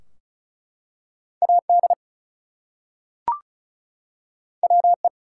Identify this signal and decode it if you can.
Morse code, heard as a keyed tone, about 36 wpm, 700 Hz. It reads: AD WE
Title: Two Letter Words - 35wpm